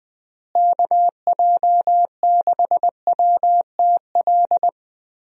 Morse code, heard as a keyed tone, about 20 wpm, 700 Hz. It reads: KJ6WTL
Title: USA Call Signs - 6 Characters - Repeated 3x - 20wpm